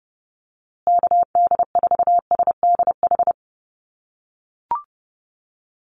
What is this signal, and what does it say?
Morse code, heard as a keyed tone, about 30 wpm, 700 Hz. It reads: KD4SDH
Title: USA Call Signs Repeated 2x 30wpm